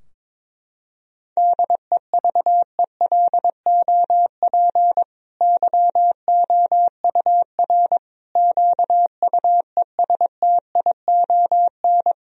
Morse code, heard as a keyed tone, about 22 wpm, 700 Hz. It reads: DEVELOP YOUR QUESTION
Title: Sentences from Top 1000 Words - Review Part 2 - 22wpm